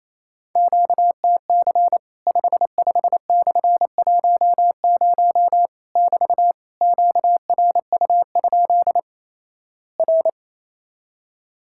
Morse code, heard as a keyed tone, about 28 wpm, 700 Hz. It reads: QTC 55/10 <BT> QRU? R
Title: Worked All Europe DX Contest - QTC Exchange - 28wpm